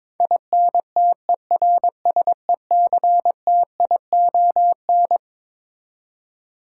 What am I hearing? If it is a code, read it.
Morse code, heard as a keyed tone, about 22 wpm, 700 Hz. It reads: INTERSECTION
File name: practice / 22wpm / Twelve Letter Words - 22wpm